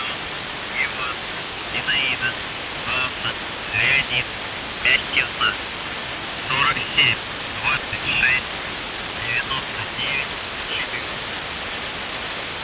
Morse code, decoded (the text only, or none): none